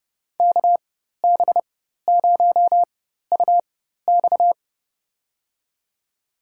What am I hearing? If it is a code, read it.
Morse code, heard as a keyed tone, about 30 wpm, spaced as for 14 wpm, 700 Hz. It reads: KB0UX
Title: USA Call Signs - 5 Characters - 4x Character Spacing - 30wpm